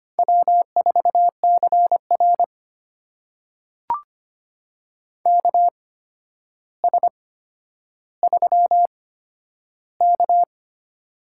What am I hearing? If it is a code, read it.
Morse code, heard as a keyed tone, about 25 wpm, spaced as for 11 wpm, 700 Hz. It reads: W4CR KS3K
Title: USA Call Signs - 4 Characters - 8x Character Spacing - 25wpm